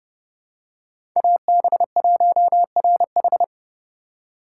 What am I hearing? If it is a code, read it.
Morse code, heard as a keyed tone, about 31 wpm, 700 Hz. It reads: AB1RH
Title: USA Call Signs - 5 Characters - 3x Character Spacing - 30wpm